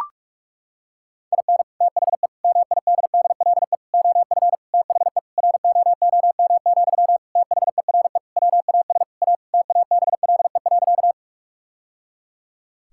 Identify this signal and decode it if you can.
Morse code, heard as a keyed tone, about 41 wpm, 700 Hz. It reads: IN THE MIDDLE OF THE ROOM, THERE WAS A TABLE.
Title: Sentences from Top 1000 Words - Review Part 6 - 45wpm